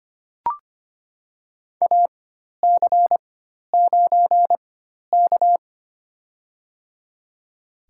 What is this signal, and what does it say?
Morse code, heard as a keyed tone, about 25 wpm, spaced as for 12 wpm, 700 Hz. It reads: AC9K
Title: USA Call Signs - 4 Characters - 4x Character Spacing - 25wpm